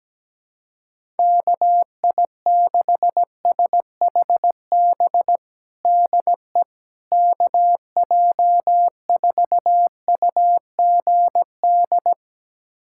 Morse code, heard as a keyed tone, about 17 wpm, 700 Hz. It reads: KI6SHB DE KJ4UGD
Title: USA Callsign DE Callsign 17wpm